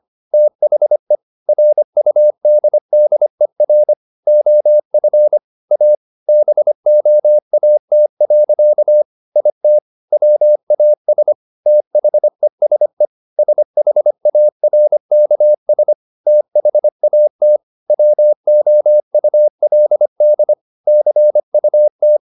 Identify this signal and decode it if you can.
Morse code, heard as a keyed tone, about 25 wpm, 600 Hz. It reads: THE RUDDER OF A BOAT. IT WAS THESE SHARKS THAT WOULD CUT